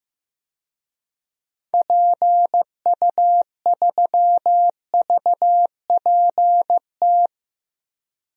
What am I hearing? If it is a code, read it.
Morse code, heard as a keyed tone, about 15 wpm, 700 Hz. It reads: PU3VPT
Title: DXCC-100 Call Signs - 6 Characters - 15wpm